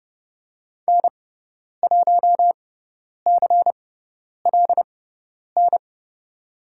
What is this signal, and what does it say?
Morse code, heard as a keyed tone, about 30 wpm, spaced as for 10 wpm, 700 Hz. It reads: N1CLN
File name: Lesson 16g - Letter c - Call Signs 30wpm